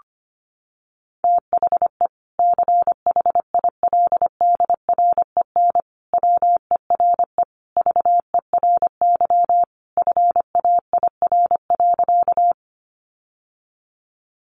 Morse code, heard as a keyed tone, about 25 wpm, 700 Hz. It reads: THE CHILDREN WERE VERY FAIR.
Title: Sentences from Top 1300 Words 25wpm